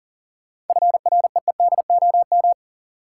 Morse code, heard as a keyed tone, about 40 wpm, 700 Hz. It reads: FREEDOM
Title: Seven Letter Words - 40wpm